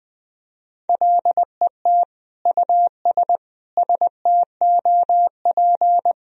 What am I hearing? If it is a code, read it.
Morse code, heard as a keyed tone, about 20 wpm, 700 Hz. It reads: LET US STOP
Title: Sentences from Top 1000 Words - Review Part 1 - 20wpm